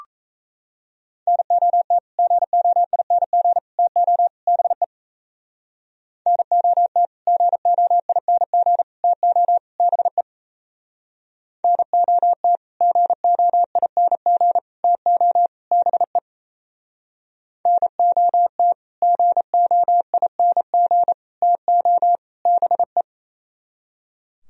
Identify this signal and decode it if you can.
Morse code, heard as a keyed tone, about 38 wpm, 700 Hz. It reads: NOT GOING TO BE NOT GOING TO BE NOT GOING TO BE NOT GOING TO BE